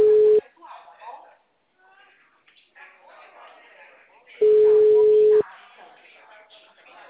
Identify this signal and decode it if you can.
Ringback tone (Europe, 425 Hz), with speech.
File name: out- -2007- - - .27780